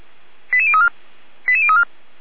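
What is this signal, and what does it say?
Five-tone selective call (ZVEI), heard as quick runs of five tones, tones of about 70 ms. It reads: 88034 88034
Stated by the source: Five-tone ZVEI selective call received on an SDR, from a fire brigade's alerting system in Germany.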